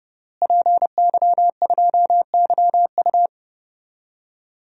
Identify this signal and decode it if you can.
Morse code, heard as a keyed tone, about 30 wpm, 700 Hz. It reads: PY2YU
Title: DXCC-100 Call Signs - 5 Characters - Repeated 3x - 30wpm